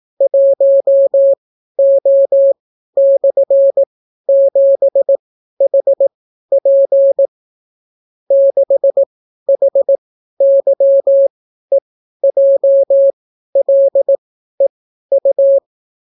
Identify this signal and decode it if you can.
Morse code, heard as a keyed tone, about 18 wpm, spaced as for 12 wpm, 550 Hz. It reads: 1O/7HP 6HYEJLEU